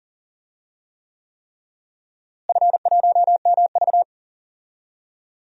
Morse code, heard as a keyed tone, about 40 wpm, 700 Hz. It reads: F1MV
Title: DXCC-100 Call Signs - 4 Characters - Repeated 3x - 40wpm